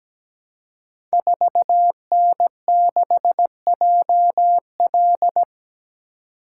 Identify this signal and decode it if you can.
Morse code, heard as a keyed tone, about 17 wpm, 700 Hz. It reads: 4N6JL
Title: Code Groups - 5 Letter:Numbers - 17wpm